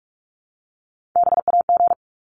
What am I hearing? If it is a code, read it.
Morse code, heard as a keyed tone, about 44 wpm, 700 Hz. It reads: BAG
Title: Top 1501-1750 Words 45wpm